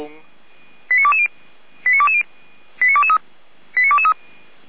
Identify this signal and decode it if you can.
Five-tone selective call (ZVEI), heard as quick runs of five tones, tones of about 70 ms, with speech. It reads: 89220 89220 89223 89223